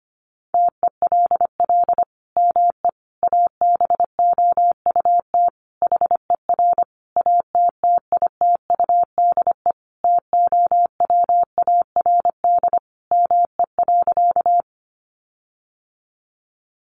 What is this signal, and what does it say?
Morse code, heard as a keyed tone, about 25 wpm, 700 Hz. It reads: TELL ME ABOUT HER ATTITUDE TOWARD ME.
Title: Sentences from Top 1300 Words 25wpm